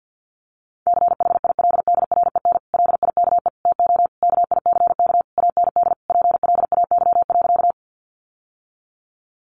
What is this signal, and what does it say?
Morse code, heard as a keyed tone, about 50 wpm, 700 Hz. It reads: CHILDREN LIKE TO KICK AND PLAY.
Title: Sentences from Top 1300 Words 50wpm